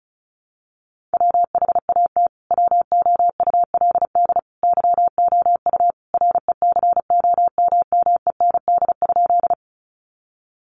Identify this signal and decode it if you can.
Morse code, heard as a keyed tone, about 34 wpm, 700 Hz. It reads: WHAT WOULD YOU RECOMMEND?
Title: Sentences from Top 1200 Words 35wpm